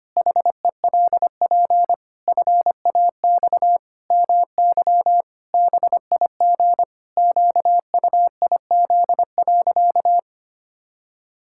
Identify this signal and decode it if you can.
Morse code, heard as a keyed tone, about 25 wpm, 700 Hz. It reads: HELP FAX MY BIG QUIZ.
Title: Pangram Challenge v2 - 25wpm